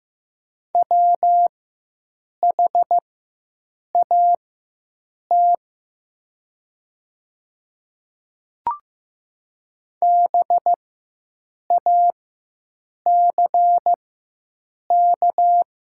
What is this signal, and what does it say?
Morse code, heard as a keyed tone, about 15 wpm, spaced as for 7 wpm, 700 Hz. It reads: WHAT BACK
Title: 4x Character Spacing - Top 200 Words - 15wpm